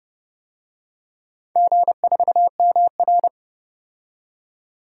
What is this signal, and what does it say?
Morse code, heard as a keyed tone, about 29 wpm, 700 Hz. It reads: G4MR
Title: DXCC-100 Call Signs - 4 Characters - 30wpm